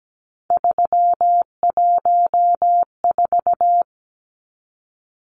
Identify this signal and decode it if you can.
Morse code, heard as a keyed tone, about 17 wpm, 700 Hz. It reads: 314
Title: Numbers - 3-Digit 17wpm